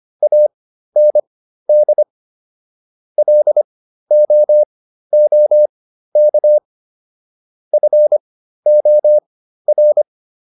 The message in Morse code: AND LOOK FOR

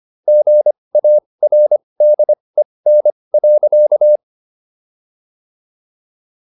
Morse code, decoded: GARDEN.

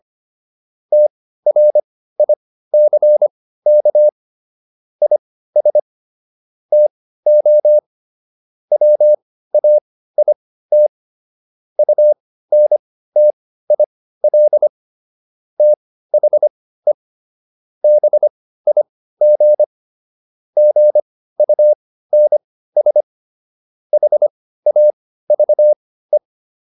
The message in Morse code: TRICK IS TO WAIT UNTIL THE BIG GUNS HAVE

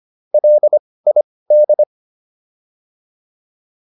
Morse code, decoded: LID

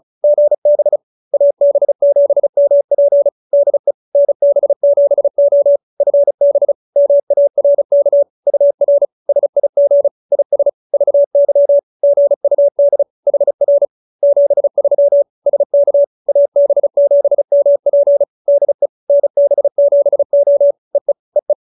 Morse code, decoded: GB AB7MP DE NB7O FB MARK UR SIG IS VY GUD HR 73 SK AB7MP DE NB7O EE EE

